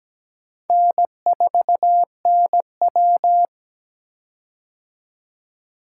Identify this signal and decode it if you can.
Morse code, heard as a keyed tone, about 17 wpm, 700 Hz. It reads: N4NW
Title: K1USN SST QSO Practice - 17wpm